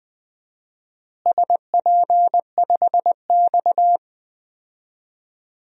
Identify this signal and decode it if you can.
Morse code, heard as a keyed tone, about 20 wpm, 700 Hz. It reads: SP5X
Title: DXCC-100 Call Signs - 4 Characters - Repeated 2x - 20wpm